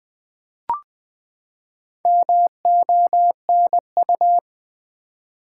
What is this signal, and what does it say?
Morse code, heard as a keyed tone, about 20 wpm, 700 Hz. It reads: MONU
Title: Build A Word - 4 Letters At A Time - 8-Letter Words - 20wpm